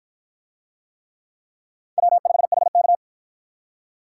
Morse code, heard as a keyed tone, about 50 wpm, 700 Hz. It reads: W5HK